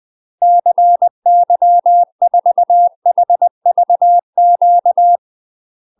Morse code, heard as a keyed tone, about 20 wpm, 700 Hz. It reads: CY4HVQ
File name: UK MASTER QSOs TRAINING / 20WPM / UK MASTER QSOs 0015